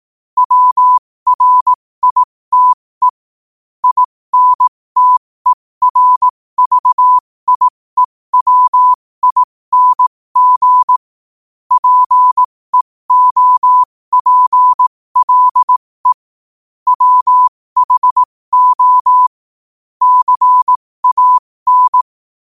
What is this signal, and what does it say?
Morse code, heard as a keyed tone, about 18 wpm, spaced as for 15 wpm, 1000 Hz. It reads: WRITE INTERVIEWING PEOPLE WHO CAN
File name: qotdfiles / 15wpm